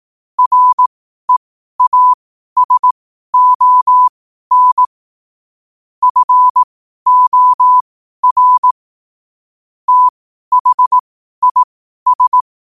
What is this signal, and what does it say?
Morse code, heard as a keyed tone, about 18 wpm, spaced as for 12 wpm, 1000 Hz. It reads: REASON FOR THIS
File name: qotdfiles / 12wpm